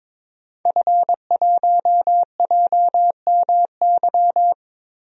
Morse code, heard as a keyed tone, about 22 wpm, 700 Hz. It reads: F1JMY
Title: DXCC-100 Call Signs - 5 Characters - Repeated 2x - 22wpm